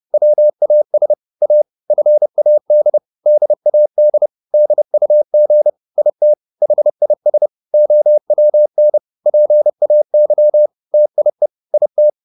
Morse code, heard as a keyed tone, about 30 wpm, 600 Hz. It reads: WAS A FAD DAD DUG IT HIS OWN PAY TIE IT